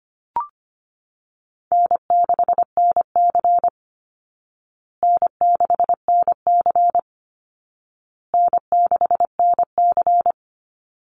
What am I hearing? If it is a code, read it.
Morse code, heard as a keyed tone, about 25 wpm, 700 Hz. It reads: N6NC N6NC N6NC
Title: Easy Call Signs Repeated 3x - 25wpm